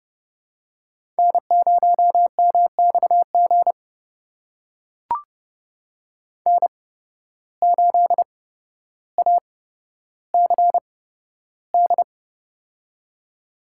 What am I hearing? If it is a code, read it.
Morse code, heard as a keyed tone, about 30 wpm, spaced as for 13 wpm, 700 Hz. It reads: N0MXG N8ACD
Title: USA Call Signs - 5 Characters - 8x Character Spacing - 30wpm